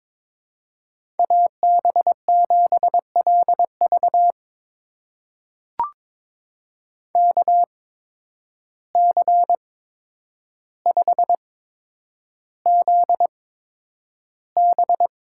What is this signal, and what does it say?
Morse code, heard as a keyed tone, about 22 wpm, spaced as for 9 wpm, 700 Hz. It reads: AB7LV KC5ZB
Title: USA Call Signs - 5 Characters - 8x Character Spacing - 22wpm